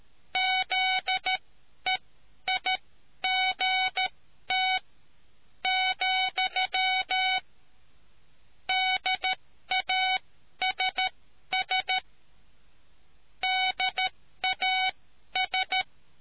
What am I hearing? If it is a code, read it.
Morse code, heard as a keyed tone, about 13 wpm, spaced as for 10 wpm, 750 Hz. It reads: ZEIGT, DASS DAS